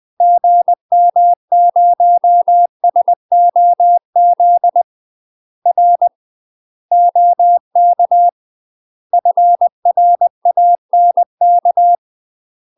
Morse code, heard as a keyed tone, about 20 wpm, 700 Hz. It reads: GM0SOZ R OK FRANK
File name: UK MASTER QSOs TRAINING/20WPM